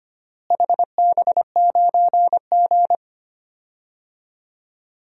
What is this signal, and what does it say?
Morse code, heard as a keyed tone, about 25 wpm, 700 Hz. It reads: HB9G
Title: DXCC-100 Call Signs - 4 Characters - Repeated 2x - 25wpm